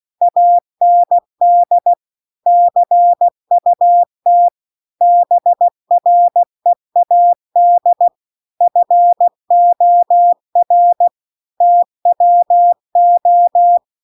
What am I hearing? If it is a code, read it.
Morse code, heard as a keyed tone, about 16 wpm, 700 Hz. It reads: AND CUT BREAD FOR TWO
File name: CW-The-Big-Twoheaded-River-2-16WPM